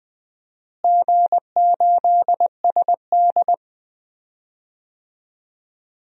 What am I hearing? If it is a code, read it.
Morse code, heard as a keyed tone, about 21 wpm, 700 Hz. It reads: G8SD
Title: DXCC-100 Call Signs - 4 Characters - Repeated 2x - 20wpm